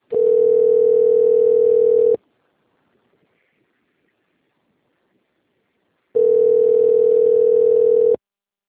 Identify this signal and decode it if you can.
Ringback tone (North America), 440 + 480 Hz.
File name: rg-1600-unknown- - - .54067